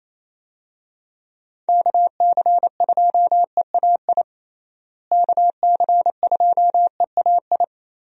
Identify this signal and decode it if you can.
Morse code, heard as a keyed tone, about 28 wpm, 700 Hz. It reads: KC2EAI KC2EAI
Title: USA Call Signs - 6 Characters - Repeated 2x - 28wpm